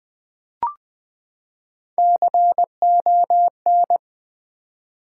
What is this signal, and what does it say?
Morse code, heard as a keyed tone, about 20 wpm, 700 Hz. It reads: CON
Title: Build A Word - 3 Letters At A Time - 8-Letter Words - 20wpm